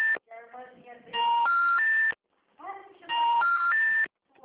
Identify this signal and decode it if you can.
Special information tone, with speech.